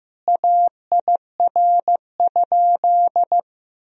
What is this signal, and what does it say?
Morse code, heard as a keyed tone, about 16 wpm, 700 Hz. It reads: AIR?
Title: Sentences from Top 1000 Words - Review Part 2 - 15wpm